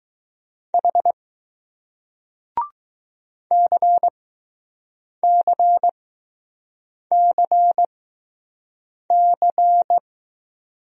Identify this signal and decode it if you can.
Morse code, heard as a keyed tone, about 20 wpm, spaced as for 6 wpm, 700 Hz. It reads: H CCCC